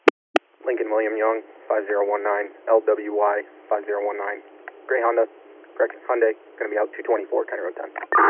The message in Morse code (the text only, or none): none